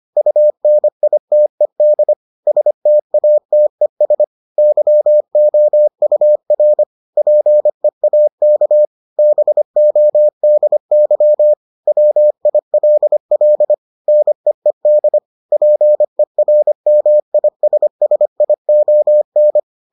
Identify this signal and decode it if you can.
Morse code, heard as a keyed tone, about 25 wpm, 600 Hz. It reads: UNITED STATES YOUR PEAK BODY WILL NEED PERMISSION